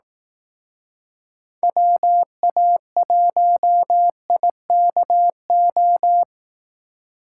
Morse code, heard as a keyed tone, about 18 wpm, 700 Hz. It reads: WA1IKO
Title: Speed Racing - USA 2x3 Call Signs 15wpm